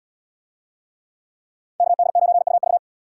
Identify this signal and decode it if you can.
Morse code, heard as a keyed tone, about 77 wpm, 700 Hz. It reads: KD9UV